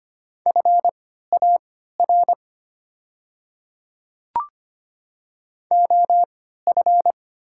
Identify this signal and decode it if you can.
Morse code, heard as a keyed tone, about 25 wpm, spaced as for 14 wpm, 700 Hz. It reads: FAR OF
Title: 3x Character Spacing - Top 300 Words - 25wpm